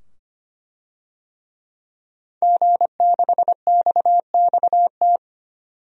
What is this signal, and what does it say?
Morse code, heard as a keyed tone, about 25 wpm, 700 Hz. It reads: G6XXT
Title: DXCC-100 Call Signs - 5 Characters - 25wpm